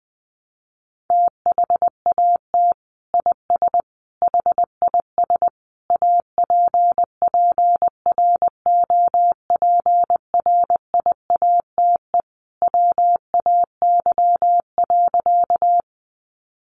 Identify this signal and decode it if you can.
Morse code, heard as a keyed tone, about 20 wpm, 700 Hz. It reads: THAT IS HIS APPROPRIATE WAY.
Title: Sentences from Top 900 Words 20wpm